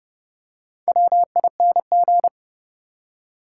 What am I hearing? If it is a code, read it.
Morse code, heard as a keyed tone, about 30 wpm, 700 Hz. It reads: WING
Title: Four Letter Words - 30wpm